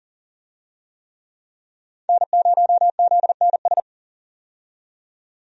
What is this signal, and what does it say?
Morse code, heard as a keyed tone, about 40 wpm, 700 Hz. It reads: N0ZNS
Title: USA Call Signs - 5 Characters - 40wpm